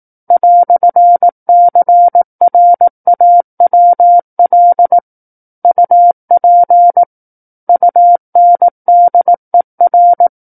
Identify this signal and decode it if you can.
Morse code, heard as a keyed tone, about 18 wpm, 700 Hz. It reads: "CRAWL UP UNDER
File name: 18 WPM